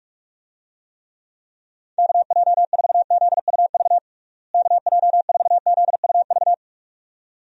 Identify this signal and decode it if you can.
Morse code, heard as a keyed tone, about 46 wpm, 700 Hz. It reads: KJ4ZUV KJ4ZUV
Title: USA Call Signs - 6 Characters - Repeated 2x - 45wpm